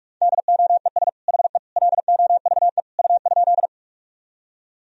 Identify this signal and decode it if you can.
Morse code, heard as a keyed tone, about 42 wpm, 700 Hz. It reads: DOES HE LOVE U?
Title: Sentences from Top 400 Words and Abbreviations 45wpm